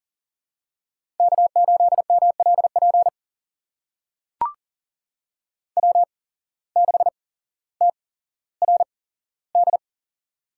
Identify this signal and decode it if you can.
Morse code, heard as a keyed tone, about 38 wpm, spaced as for 17 wpm, 700 Hz. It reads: K8MLP W6TRD
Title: USA Call Signs - 5 Characters - 8x Character Spacing - 40wpm